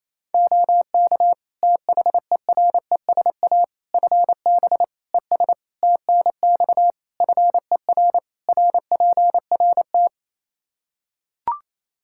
Morse code, heard as a keyed tone, about 28 wpm, 700 Hz. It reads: OK THERESA FB ES TNX FER RPRT
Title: Practice QSO - Exchange 2 - Info - 28wpm